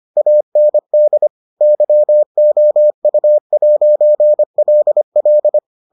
Morse code, heard as a keyed tone, about 25 wpm, 600 Hz. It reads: AND YOU'LL